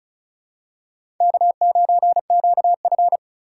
Morse code, heard as a keyed tone, about 36 wpm, 700 Hz. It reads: K9QF
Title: USA Call Signs - 4 Characters - 4x Character Spacing - 35wpm